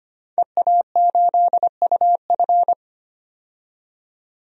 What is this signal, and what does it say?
Morse code, heard as a keyed tone, about 25 wpm, 700 Hz. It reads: EA8UF
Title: DXCC-100 Call Signs - 5 Characters - 25wpm